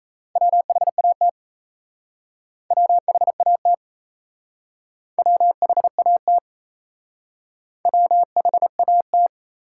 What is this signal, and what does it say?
Morse code, heard as a keyed tone, about 34 wpm, 700 Hz. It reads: WHAT WHAT WHAT WHAT
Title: Speed-Racing Top 180 QSO Elements 28wpm